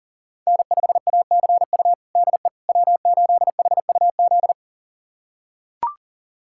Morse code, heard as a keyed tone, about 40 wpm, 700 Hz. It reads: N5ACU DE W8HUZ
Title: USA Callsign DE Callsign - 5 Characters - 40wpm